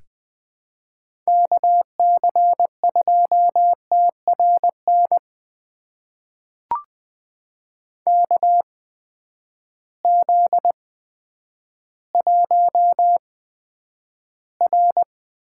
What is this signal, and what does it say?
Morse code, heard as a keyed tone, about 21 wpm, spaced as for 10 wpm, 700 Hz. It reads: KC2TRN KZ1R
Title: USA Call Signs - 8x Character Spacing - 20wpm